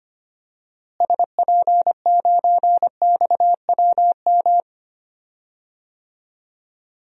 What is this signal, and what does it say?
Morse code, heard as a keyed tone, about 25 wpm, 700 Hz. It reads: SP9XWM
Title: DXCC-100 Call Signs - 6 Characters - Repeated 3x - 25wpm